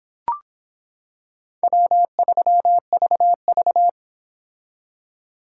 Morse code, heard as a keyed tone, about 26 wpm, 700 Hz. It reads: W3VV